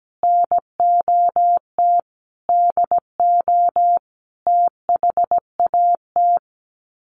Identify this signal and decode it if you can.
Morse code, heard as a keyed tone, about 17 wpm, 700 Hz. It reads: NOT DO THAT
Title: Sets of 5 Words 17wpm